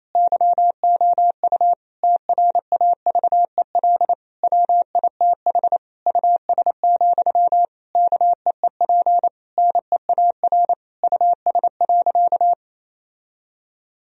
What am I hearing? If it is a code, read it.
Morse code, heard as a keyed tone, about 28 wpm, 700 Hz. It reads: YOU TRAVEL WITH US, KEEP NEAR US.
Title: Sentences from Top 1000 Words - Review Part 2 - 28wpm